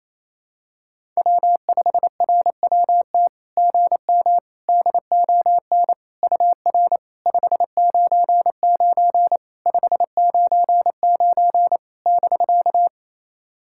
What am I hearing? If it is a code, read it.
Morse code, heard as a keyed tone, about 28 wpm, 660 Hz. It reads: W5RWT GM DON UR 599 599 <BK>
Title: SOTA QSO Practice 28wpm